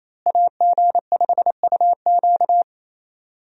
Morse code, heard as a keyed tone, about 28 wpm, 700 Hz. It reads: AG5UQ
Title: USA Call Signs - 5 Characters - Repeated 3x - 28wpm